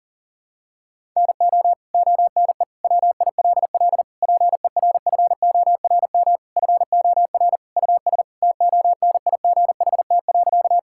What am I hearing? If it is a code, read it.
Morse code, heard as a keyed tone, about 40 wpm, 700 Hz. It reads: NO ONE WILL PERFORM FOR US TONIGHT.
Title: Sentences from Top 1000 Words - Review Part 5 - 40wpm